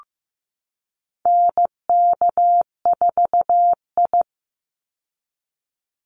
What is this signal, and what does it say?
Morse code, heard as a keyed tone, about 15 wpm, 700 Hz. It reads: NK4I
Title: Call Signs Repeated 2x - 15wpm